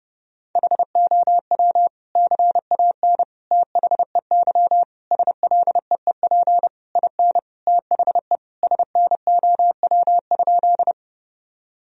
Morse code, heard as a keyed tone, about 30 wpm, 700 Hz. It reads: HOW CAN THEY SLEEP IN THE SNOW?
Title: Sentences from Top 1000 Words - Review Part 3 - 30wpm